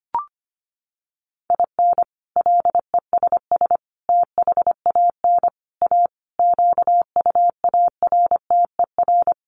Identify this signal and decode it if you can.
Morse code, heard as a keyed tone, about 25 wpm, 700 Hz. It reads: IN LESS THAN A QUARTER
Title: Sentences from Top 1300 Words 25wpm